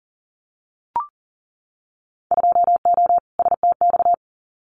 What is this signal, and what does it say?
Morse code, heard as a keyed tone, about 38 wpm, 700 Hz. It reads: 2O STX
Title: Winter Field Day Exchanges 40wpm